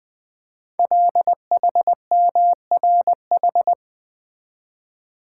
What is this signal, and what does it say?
Morse code, heard as a keyed tone, about 20 wpm, 700 Hz. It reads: LHMRH
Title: Code Groups - 5 Letters - 20wpm